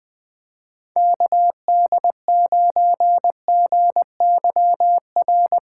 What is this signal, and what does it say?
Morse code, heard as a keyed tone, about 20 wpm, 700 Hz. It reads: KD9GYR
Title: USA Callsign DE Callsign - 6 Characters - 20wpm